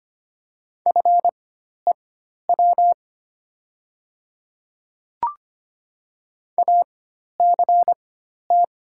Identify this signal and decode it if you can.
Morse code, heard as a keyed tone, about 25 wpm, spaced as for 12 wpm, 700 Hz. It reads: FEW ACT
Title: 4x Character Spacing - Top 300 Words - 25wpm